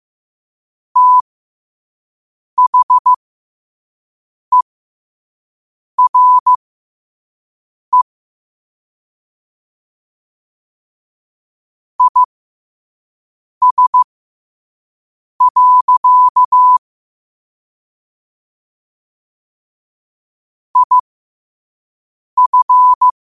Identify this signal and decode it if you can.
Morse code, heard as a keyed tone, about 15 wpm, spaced as for 5 wpm, 1000 Hz. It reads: THERE IS. IF
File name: qotdfiles / 5wpm